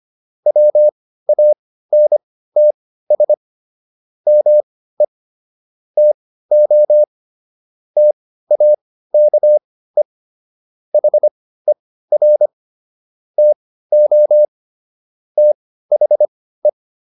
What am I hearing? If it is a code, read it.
Morse code, heard as a keyed tone, about 25 wpm, spaced as for 15 wpm, 600 Hz. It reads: WANTS ME TO TAKE HER TO THE